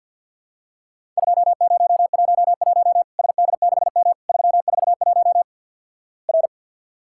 Morse code, heard as a keyed tone, about 50 wpm, 700 Hz. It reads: 2011 SD6M 341 R
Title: Worked All Europe DX Contest - QTC Exchange - 50wpm